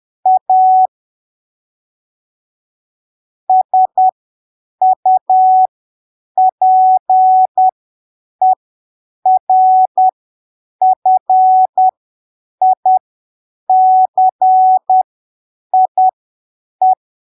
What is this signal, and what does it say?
Morse code, heard as a keyed tone, about 10 wpm, spaced as for 7 wpm, 750 Hz. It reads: A SUPERFICIE
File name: arquivos / texto / texto 10 10ppm